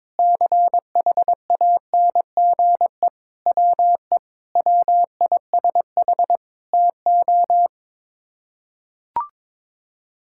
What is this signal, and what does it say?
Morse code, heard as a keyed tone, about 22 wpm, 700 Hz. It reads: CHANGE WE WISH TO
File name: practice / 22wpm / Sets of 8 Words - 22wpm